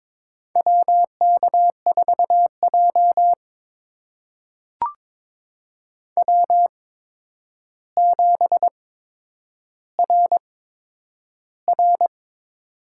Morse code, heard as a keyed tone, about 22 wpm, spaced as for 9 wpm, 700 Hz. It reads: WK4J W7RR
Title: USA Call Signs - 4 Characters - 8x Character Spacing - 22wpm